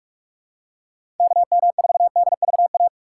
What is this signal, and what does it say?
Morse code, heard as a keyed tone, about 47 wpm, 700 Hz. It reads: KM4DVA
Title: USA Call Signs - 6 Characters - 6x Character Spacing - 45wpm